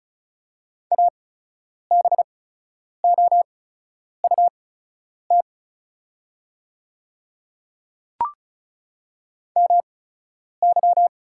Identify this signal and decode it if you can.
Morse code, heard as a keyed tone, about 34 wpm, spaced as for 10 wpm, 700 Hz. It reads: ABOUT MY
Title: 8x Character Spacing - Top 100 Words - 35wpm